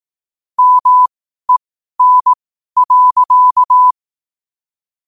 Morse code, heard as a keyed tone, about 18 wpm, spaced as for 12 wpm, 1000 Hz. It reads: MEN.
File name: qotdfiles / 12wpm